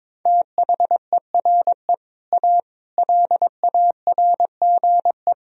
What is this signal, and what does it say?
Morse code, heard as a keyed tone, about 22 wpm, 700 Hz. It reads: THERE A LARGE
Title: Sentences from Top 1000 Words - Review Part 3 - 22wpm